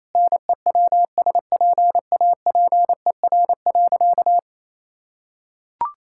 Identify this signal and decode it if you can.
Morse code, heard as a keyed tone, about 28 wpm, 700 Hz. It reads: NEWSPAPER.
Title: Sentences from Top 1000 Words - Review Part 5 - 28wpm